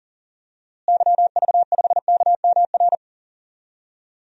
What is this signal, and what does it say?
Morse code, heard as a keyed tone, about 40 wpm, 700 Hz. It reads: YV5KMR
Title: DXCC-100 Call Signs - 6 Characters - Repeated 3x - 40wpm